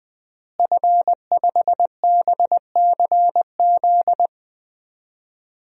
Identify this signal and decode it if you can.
Morse code, heard as a keyed tone, about 20 wpm, 700 Hz. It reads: F5BCZ
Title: DXCC-100 Call Signs - 5 Characters - Repeated 3x - 20wpm